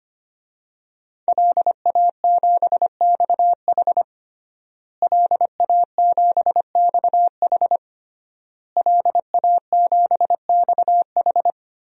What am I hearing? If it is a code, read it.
Morse code, heard as a keyed tone, about 25 wpm, 700 Hz. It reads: LA7XH LA7XH LA7XH
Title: DXCC-100 Call Signs - 5 Characters - Repeated 3x - 25wpm